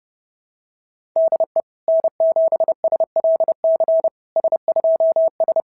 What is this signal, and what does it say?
Morse code, heard as a keyed tone, about 26 wpm, 650 Hz. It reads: DE N7SLC S2S
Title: SOTA QSO Practice 30wpm